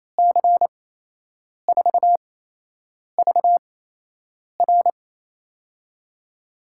Morse code, heard as a keyed tone, about 28 wpm, spaced as for 8 wpm, 700 Hz. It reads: C4VR
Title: USA Call Signs - 5 Characters - 8x Character Spacing - 28wpm